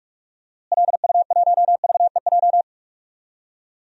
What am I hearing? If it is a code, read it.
Morse code, heard as a keyed tone, about 46 wpm, 700 Hz. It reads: LU1VEJ
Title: DXCC-100 Call Signs - 6 Characters - Repeated 3x - 45wpm